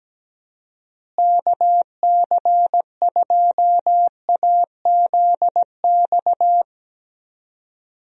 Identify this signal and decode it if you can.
Morse code, heard as a keyed tone, about 17 wpm, 700 Hz. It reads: KC2AZX
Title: USA Call Signs - 6 Characters - Repeated 3x - 17wpm